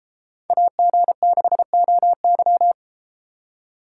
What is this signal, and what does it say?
Morse code, heard as a keyed tone, about 31 wpm, 700 Hz. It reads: AG6OY